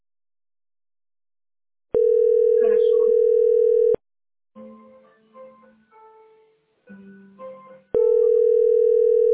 Ringback tone (North America), 440 + 480 Hz.